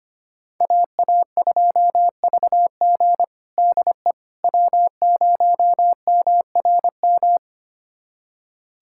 Morse code, heard as a keyed tone, about 25 wpm, 700 Hz. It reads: AA2VG DE W0MRM
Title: USA Callsign DE Callsign - 5 Characters - 25wpm